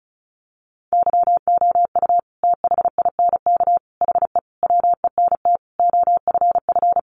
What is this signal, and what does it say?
Morse code, heard as a keyed tone, about 34 wpm, 700 Hz. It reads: YOU THINK HE WENT OFF
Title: Sentences from Top 800 Words 35wpm